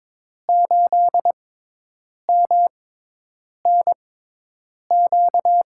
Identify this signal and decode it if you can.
Morse code, heard as a keyed tone, about 22 wpm, spaced as for 8 wpm, 700 Hz. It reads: 8MNQ
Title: USA Call Signs - 5 Characters - 6x Character Spacing - 22wpm